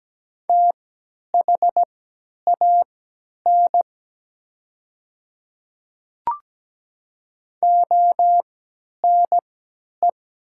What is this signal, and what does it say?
Morse code, heard as a keyed tone, about 17 wpm, spaced as for 10 wpm, 700 Hz. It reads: THAN ONE
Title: 3x Character Spacing - Top 100 Words - 17wpm